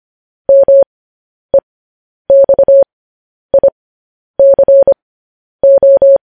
Morse code, heard as a keyed tone, about 25 wpm, spaced as for 10 wpm, 550 Hz. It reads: MEXICO
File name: podcast / News headlines-1x-25x10